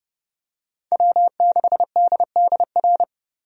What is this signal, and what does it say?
Morse code, heard as a keyed tone, about 30 wpm, 700 Hz. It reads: W6DDR
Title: USA Call Signs - 5 Characters - Repeated 3x - 30wpm